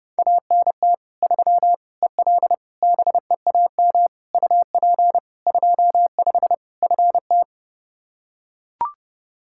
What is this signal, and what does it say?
Morse code, heard as a keyed tone, about 32 wpm, 700 Hz. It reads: ANT 3 EL BEAM UP 25 FT
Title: Practice QSO - Exchange 2 - Info - 30wpm